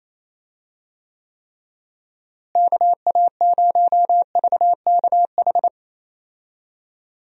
Morse code, heard as a keyed tone, about 28 wpm, 700 Hz. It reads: KA0VKH
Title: USA Call Signs - 6 Characters - Repeated 3x - 28wpm